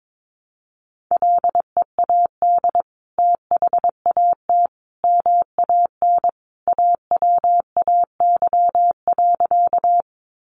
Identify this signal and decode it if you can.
Morse code, heard as a keyed tone, about 22 wpm, 700 Hz. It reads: LEAD THAT MAN AWAY.
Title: Sentences from Top 600 Words 22wpm